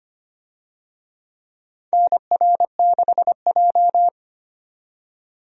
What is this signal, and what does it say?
Morse code, heard as a keyed tone, about 25 wpm, 700 Hz. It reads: NR6J
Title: USA Call Signs - 4 Characters - 25wpm